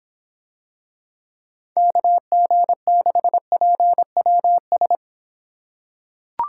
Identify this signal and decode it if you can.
Morse code, heard as a keyed tone, about 25 wpm, 700 Hz. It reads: KG6PWS